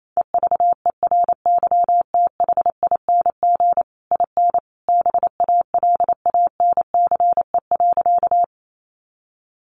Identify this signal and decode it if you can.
Morse code, heard as a keyed tone, about 28 wpm, 700 Hz. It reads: EVERYTHING IN BALANCE.
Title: A Sound Of Thunder 28wpm